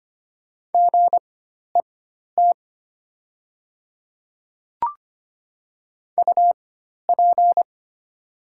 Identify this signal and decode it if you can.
Morse code, heard as a keyed tone, about 25 wpm, spaced as for 12 wpm, 690 Hz. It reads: GET UP
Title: 4x Character Spacing - Top 200 Words - 25wpm